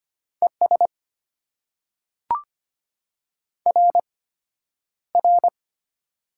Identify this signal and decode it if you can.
Morse code, heard as a keyed tone, about 25 wpm, spaced as for 11 wpm, 700 Hz. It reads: ES RR
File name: Top 90 QSO Elements - 8x Character Spacing - 25wpm